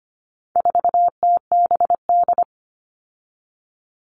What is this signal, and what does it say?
Morse code, heard as a keyed tone, about 25 wpm, 700 Hz. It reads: 4TBD
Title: All Cut Numbers - 4-Digit 25wpm